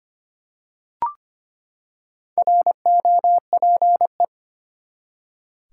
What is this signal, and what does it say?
Morse code, heard as a keyed tone, about 25 wpm, 700 Hz. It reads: ROPE
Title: Four Letter Words - 25wpm